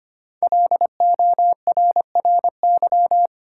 Morse code, heard as a keyed tone, about 25 wpm, 700 Hz. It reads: LORRY
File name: 5 Letter Words - First 3 Letters Given - 25wpm